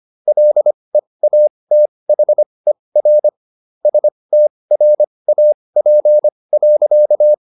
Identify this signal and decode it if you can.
Morse code, heard as a keyed tone, about 25 wpm, spaced as for 20 wpm, 600 Hz. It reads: LEATHER STRAP.